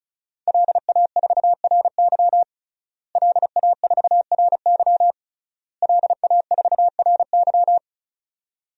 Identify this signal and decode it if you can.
Morse code, heard as a keyed tone, about 34 wpm, 700 Hz. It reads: LA4RY LA4RY LA4RY
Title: DXCC-100 Call Signs - 5 Characters - Repeated 3x - 35wpm